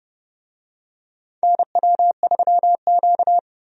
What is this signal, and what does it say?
Morse code, heard as a keyed tone, about 30 wpm, 700 Hz. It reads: NW3Q
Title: USA Call Signs - 4 Characters - 30wpm